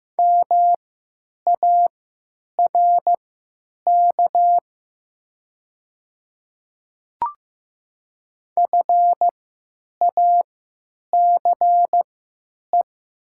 Morse code, heard as a keyed tone, about 15 wpm, spaced as for 9 wpm, 700 Hz. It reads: MARK FACE
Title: 3x Character Spacing - Top 500 Words - 15wpm